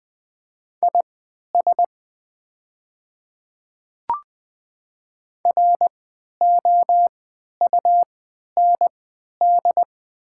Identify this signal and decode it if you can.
Morse code, heard as a keyed tone, about 20 wpm, spaced as for 11 wpm, 700 Hz. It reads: IS ROUND
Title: 3x Character Spacing - Top 300 Words - 20wpm